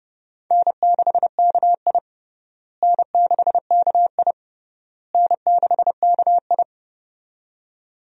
Morse code, heard as a keyed tone, about 30 wpm, 700 Hz. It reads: N6KI N6KI N6KI
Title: USA Call Signs - 4 Characters - Repeated 3x - 30wpm